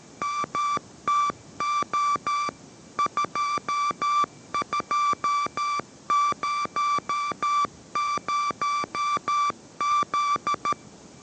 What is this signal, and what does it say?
Morse code, heard as a keyed tone, about 13 wpm, 1170 Hz. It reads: MTO 2200Z